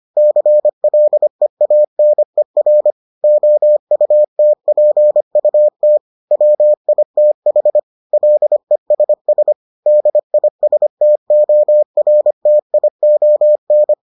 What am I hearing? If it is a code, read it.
Morse code, heard as a keyed tone, about 25 wpm, 600 Hz. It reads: CLEANER OUTPUT WITH LESS DISTORTION